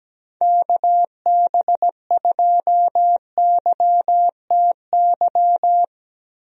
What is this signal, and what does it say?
Morse code, heard as a keyed tone, about 17 wpm, 700 Hz. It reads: KB2YTY
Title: USA Call Signs - 6 Characters - Repeated 3x - 17wpm